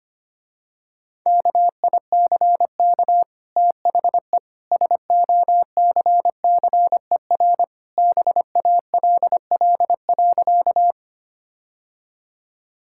Morse code, heard as a keyed tone, about 25 wpm, 700 Hz. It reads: KICK THE SOCCER BALL.
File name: Easy Sentences with Hints - 25wpm